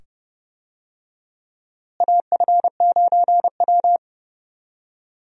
Morse code, heard as a keyed tone, about 30 wpm, 700 Hz. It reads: AF9W
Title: USA Call Signs - 4 Characters - Repeated 2x - 30wpm